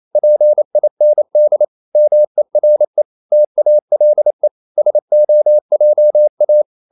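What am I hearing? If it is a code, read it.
Morse code, heard as a keyed tone, about 28 wpm, 600 Hz. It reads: PIND MERE TALE SOJA
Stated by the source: Danish short words sent in Morse at 28 wpm, from CW med Gnister.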